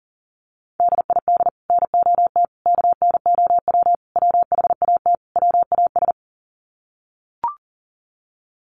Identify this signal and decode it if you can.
Morse code, heard as a keyed tone, about 40 wpm, 700 Hz. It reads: DID NOT KNOW WHAT WAS
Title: Sets of 5 Words Encore2 40wpm